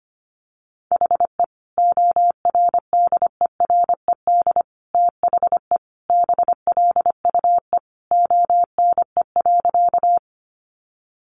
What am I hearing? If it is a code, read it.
Morse code, heard as a keyed tone, about 25 wpm, 700 Hz. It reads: HE ORDERED THE BLUE ONE.
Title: Additional Easy Phrases 25wpm